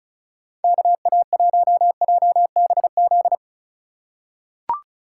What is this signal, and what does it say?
Morse code, heard as a keyed tone, about 34 wpm, 700 Hz. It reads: KA1JBZ
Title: USA Call Signs - 6 Characters - 2x Character Spacing - 35wpm